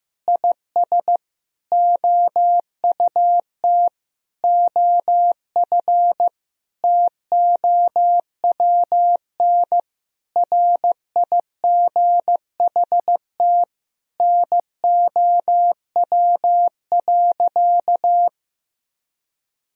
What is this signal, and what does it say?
Morse code, heard as a keyed tone, about 15 wpm, 700 Hz. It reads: IS OUT OF TOWN RIGHT NOW.
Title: Sentences from Top 1000 Words - Review Part 4 - 15wpm